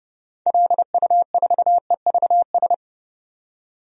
Morse code, heard as a keyed tone, about 30 wpm, 700 Hz. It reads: LU4EVS
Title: DXCC-100 Call Signs - 6 Characters - 30wpm